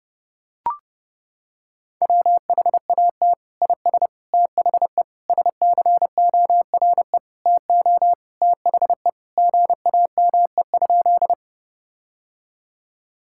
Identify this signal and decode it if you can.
Morse code, heard as a keyed tone, about 30 wpm, 700 Hz. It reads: WHAT IS THE SCORE TO THE GAME?
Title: Sentences from Top 1600 Words 30wpm